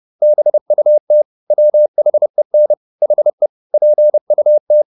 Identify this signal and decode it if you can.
Morse code, heard as a keyed tone, about 30 wpm, 600 Hz. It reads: BUT WHEN HE PUT